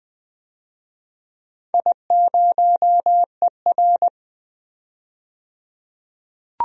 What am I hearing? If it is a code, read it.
Morse code, heard as a keyed tone, about 20 wpm, 700 Hz. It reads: I0ER
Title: DXCC-100 Call Signs - 4 Characters - 20wpm